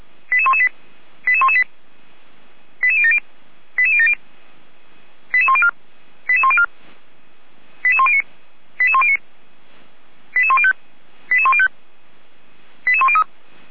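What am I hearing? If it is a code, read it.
Five-tone selective call (ZVEI), heard as quick runs of five tones, tones of about 70 ms. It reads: 88118 88118 88070 88070 88104 88104 88109 88109 88105 88105 88103
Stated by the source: Five-tone ZVEI selective call received by SDR, from feuerwehr-orlamuende.org (Orlamünde fire brigade) server.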